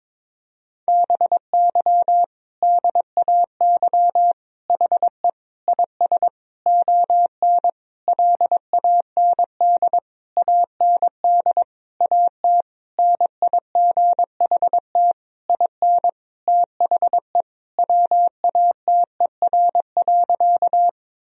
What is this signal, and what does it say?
Morse code, heard as a keyed tone, about 21 wpm, 700 Hz. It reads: BY DAY HE IS ON LAND AND AT NIGHT IN THE WATER.
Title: Sentences from Top 1000 Words - Review Part 1 - 22wpm